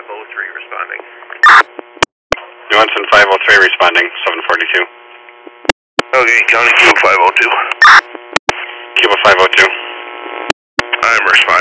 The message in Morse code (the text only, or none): none